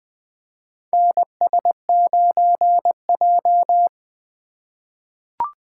Morse code, heard as a keyed tone, about 20 wpm, 700 Hz. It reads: NS9J
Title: USA Call Signs - 4 Characters - 8x Character Spacing - 20wpm